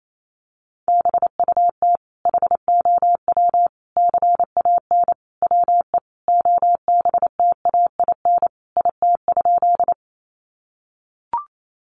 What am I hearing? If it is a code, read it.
Morse code, heard as a keyed tone, about 28 wpm, 700 Hz. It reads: BUT HOW CAN WE OBTAIN IT?
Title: Sentences from Top 1300 Words 28wpm